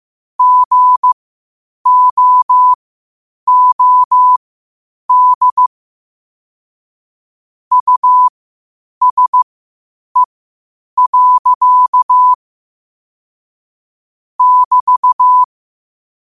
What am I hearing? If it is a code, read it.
Morse code, heard as a keyed tone, about 15 wpm, spaced as for 8 wpm, 1000 Hz. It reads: GOOD USE. <BT>
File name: qotdfiles / 8wpm